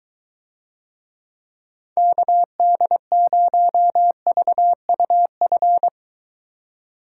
Morse code, heard as a keyed tone, about 23 wpm, 700 Hz. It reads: KD0VUF